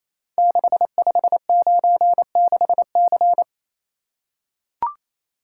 Morse code, heard as a keyed tone, about 28 wpm, 700 Hz. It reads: 6596C
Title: Code Groups - 5 Letter:Numbers - 28wpm